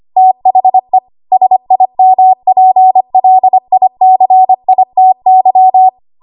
Morse code, heard as a keyed tone, about 25 wpm, 750 Hz. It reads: THE SIMPLICITY
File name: Archive / 25 WPM / 25WPM